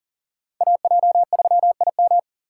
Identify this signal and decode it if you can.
Morse code, heard as a keyed tone, about 40 wpm, 700 Hz. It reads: AJ3IM
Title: USA Call Signs - 8x Character Spacing - 40wpm